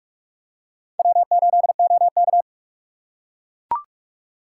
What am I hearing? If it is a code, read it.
Morse code, heard as a keyed tone, about 44 wpm, 700 Hz. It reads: W8OK